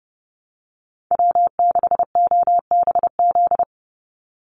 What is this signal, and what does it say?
Morse code, heard as a keyed tone, about 30 wpm, 700 Hz. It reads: W6OBZ
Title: USA Call Signs 30wpm